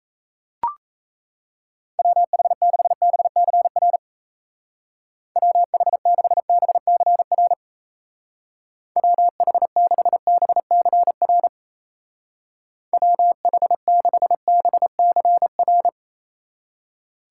Morse code, heard as a keyed tone, about 35 wpm, 700 Hz. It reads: WH6BCR WH6BCR WH6BCR WH6BCR